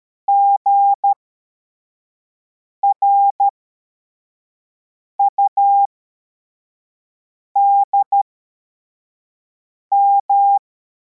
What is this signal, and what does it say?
Morse code, heard as a keyed tone, about 13 wpm, spaced as for 4 wpm, 800 Hz. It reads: GRUDM